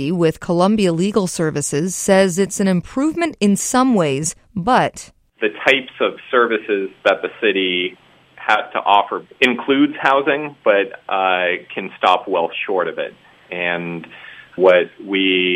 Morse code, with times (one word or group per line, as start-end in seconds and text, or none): none